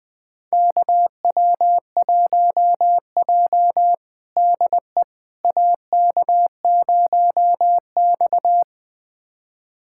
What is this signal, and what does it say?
Morse code, heard as a keyed tone, about 21 wpm, 700 Hz. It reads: KW1J DE AK0X
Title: USA Callsign DE Callsign - 4 Characters - 20wpm